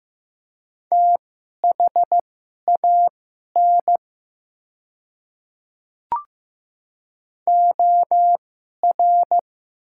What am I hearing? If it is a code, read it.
Morse code, heard as a keyed tone, about 15 wpm, spaced as for 11 wpm, 700 Hz. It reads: THAN OR
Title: 2x Character Spacing - Top 300 Words - 15wpm